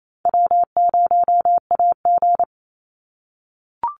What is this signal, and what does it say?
Morse code, heard as a keyed tone, about 28 wpm, 700 Hz. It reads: W0AG
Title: Call Signs - 28wpm